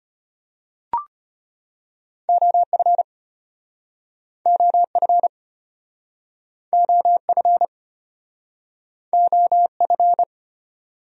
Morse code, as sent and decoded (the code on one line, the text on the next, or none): --- ..-. / --- ..-. / --- ..-. / --- ..-.
OF OF OF OF